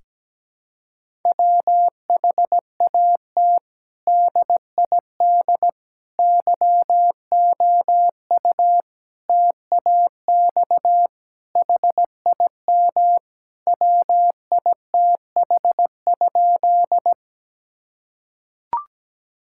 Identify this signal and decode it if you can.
Morse code, heard as a keyed tone, about 17 wpm, 700 Hz. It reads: WHAT DID YOU TAX HIM WITH?
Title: Sentences from Top 1000 Words - Review Part 5 - 17wpm